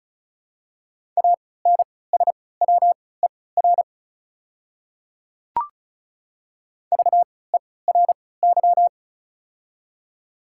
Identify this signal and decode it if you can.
Morse code, heard as a keyed tone, about 34 wpm, spaced as for 20 wpm, 700 Hz. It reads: ANSWER VERY
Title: 3x Character Spacing - Top 500 Words - 35wpm